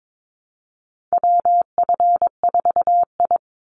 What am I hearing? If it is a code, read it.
Morse code, heard as a keyed tone, about 22 wpm, 700 Hz. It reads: WF4I
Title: NA SOTA Chaser Callsigns 22wpm